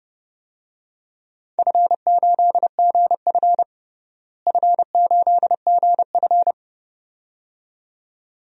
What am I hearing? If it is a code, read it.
Morse code, heard as a keyed tone, about 30 wpm, 700 Hz. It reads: F8GF F8GF
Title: DXCC-100 Call Signs - 4 Characters - Repeated 2x - 30wpm